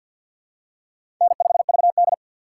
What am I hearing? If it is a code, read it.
Morse code, heard as a keyed tone, about 51 wpm, 700 Hz. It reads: N5VD
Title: USA Call Signs - 4 Characters - 3x Character Spacing - 50wpm